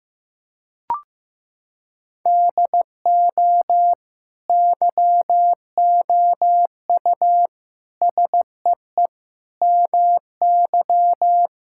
Morse code, heard as a keyed tone, about 15 wpm, 700 Hz. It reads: DO YOU SEE MY
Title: Sentences from Top 1600 Words 15wpm